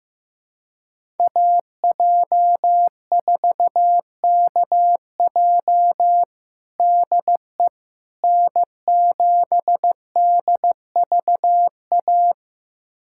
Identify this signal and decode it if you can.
Morse code, heard as a keyed tone, about 15 wpm, 700 Hz. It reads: AJ4KJ DE N7DVA
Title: USA Callsign DE Callsign - 5 Characters - 15wpm